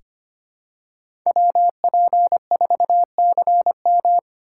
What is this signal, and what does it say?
Morse code, heard as a keyed tone, about 25 wpm, 700 Hz. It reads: WP4CM
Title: USA Call Signs - 5 Characters - 8x Character Spacing - 25wpm